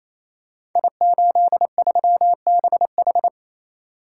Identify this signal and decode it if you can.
Morse code, heard as a keyed tone, about 29 wpm, 700 Hz. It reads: I83BH
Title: Code Groups - 5 Letter:Numbers - 28wpm